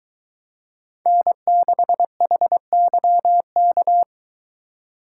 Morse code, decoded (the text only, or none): N6HYK